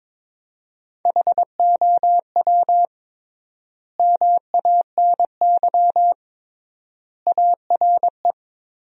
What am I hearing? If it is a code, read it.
Morse code, heard as a keyed tone, about 22 wpm, 700 Hz. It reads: HOW MANY ARE
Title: 3x Word-Spacing - Sentences from Top 300 Words - 22wpm